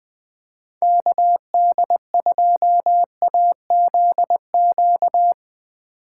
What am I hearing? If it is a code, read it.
Morse code, heard as a keyed tone, about 21 wpm, 700 Hz. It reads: KD2AZQ
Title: USA Call Signs - 6 Characters - 2x Character Spacing - 20wpm